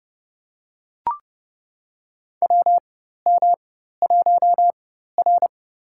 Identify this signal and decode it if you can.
Morse code, heard as a keyed tone, about 30 wpm, spaced as for 14 wpm, 700 Hz. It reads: WM1R
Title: USA Call Signs - 4 Characters - 4x Character Spacing - 30wpm